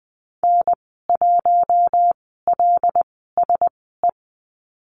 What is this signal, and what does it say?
Morse code, heard as a keyed tone, about 20 wpm, spaced as for 15 wpm, 700 Hz. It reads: N1LSE
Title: 076 Lesson 13g - Letter L - Call Signs 20wpm